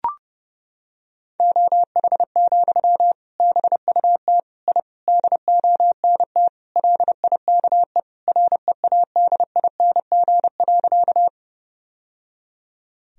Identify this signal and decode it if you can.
Morse code, heard as a keyed tone, about 32 wpm, 700 Hz. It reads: OH, BUT I DONT LIKE READING.
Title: Sentences from Top 1000 Words - Review Part 3 - 30wpm